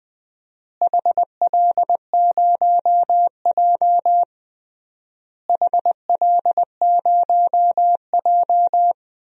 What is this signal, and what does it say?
Morse code, heard as a keyed tone, about 21 wpm, 700 Hz. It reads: HL0J HL0J
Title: DXCC-100 Call Signs - 4 Characters - Repeated 3x - 20wpm